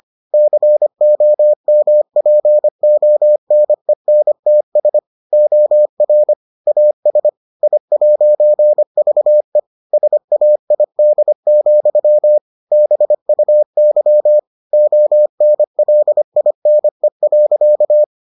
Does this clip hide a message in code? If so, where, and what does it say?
Yes: Morse code from 0.3 to 18.0 s, heard as a keyed tone, about 25 wpm, 600 Hz. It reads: COMPONENTS OR AS I'VE SAID, BUY ONLINE.